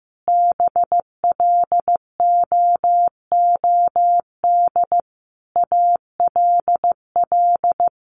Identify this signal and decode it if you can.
Morse code, heard as a keyed tone, about 15 wpm, 700 Hz. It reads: BLOOD ALL
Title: Sentences from Top 1100 Words 15wpm